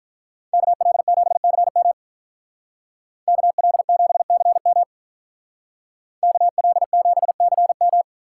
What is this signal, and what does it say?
Morse code, heard as a keyed tone, about 46 wpm, 700 Hz. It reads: KL7CM KL7CM KL7CM